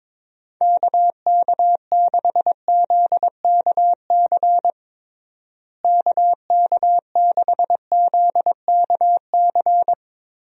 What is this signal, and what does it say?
Morse code, heard as a keyed tone, about 22 wpm, 700 Hz. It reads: KK6ZKC KK6ZKC
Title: USA Call Signs - 6 Characters - Repeated 3x - 22wpm